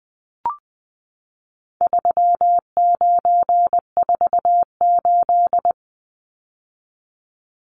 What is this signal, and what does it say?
Morse code, heard as a keyed tone, about 20 wpm, 700 Hz. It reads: 3948
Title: Numbers - 4-Digit 20wpm